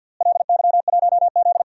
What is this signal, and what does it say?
Morse code, heard as a keyed tone, about 49 wpm, 700 Hz. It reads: LY1Z
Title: DXCC-100 Call Signs - 4 Characters - Repeated 3x - 50wpm